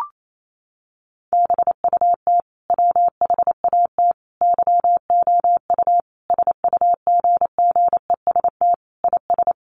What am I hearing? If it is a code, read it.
Morse code, heard as a keyed tone, about 28 wpm, 700 Hz. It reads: BUT WHAT YOU SUGGEST IS
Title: Sentences from Top 1300 Words 28wpm